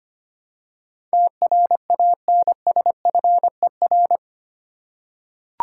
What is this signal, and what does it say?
Morse code, heard as a keyed tone, about 25 wpm, 700 Hz. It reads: TRANSFER
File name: 8 Letter Words - First 3 Letters Given - 25wpm